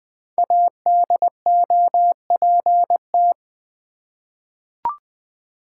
Morse code, heard as a keyed tone, about 20 wpm, 700 Hz. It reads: ADOPT
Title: Five Letter Words - 20wpm